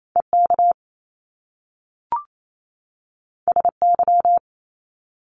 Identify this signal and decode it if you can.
Morse code, heard as a keyed tone, about 28 wpm, 700 Hz. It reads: EK SY
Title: Top Two Letter Combinations in Words 28wpm